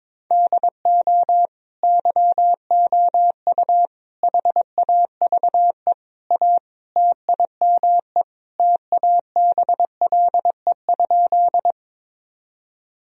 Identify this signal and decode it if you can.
Morse code, heard as a keyed tone, about 22 wpm, 700 Hz. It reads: DO YOU HAVE A TIME TABLE?
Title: Sentences from Top 1000 Words - Review Part 2 - 22wpm